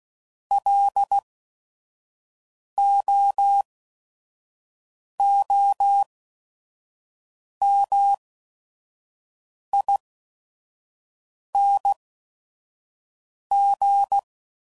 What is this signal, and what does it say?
Morse code, heard as a keyed tone, about 16 wpm, spaced as for 5 wpm, 800 Hz. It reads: LOOMING
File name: Prefixes / By First Letter / Ls